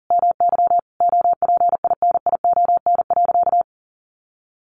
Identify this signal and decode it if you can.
Morse code, heard as a keyed tone, about 40 wpm, 700 Hz. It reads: MY OPINION.
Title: Sentences from Top 1200 Words 40wpm